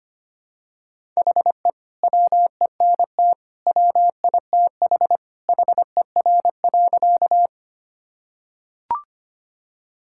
Morse code, heard as a keyed tone, about 25 wpm, 700 Hz. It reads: HE WENT WITH HER.
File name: practice / 25wpm / Sentences with 2-Word Phrases - 25wpm